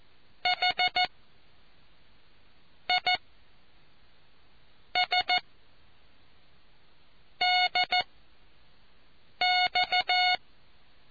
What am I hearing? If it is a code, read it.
Morse code, heard as a keyed tone, about 14 wpm, 730 Hz. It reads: H I S D X